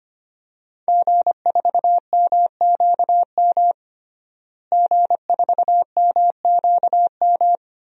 Morse code, heard as a keyed tone, about 25 wpm, 700 Hz. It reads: G4MQM G4MQM
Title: DXCC-100 Call Signs - 5 Characters - Repeated 3x - 25wpm